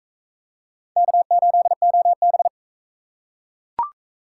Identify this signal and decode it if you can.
Morse code, heard as a keyed tone, about 42 wpm, 700 Hz. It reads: K8OB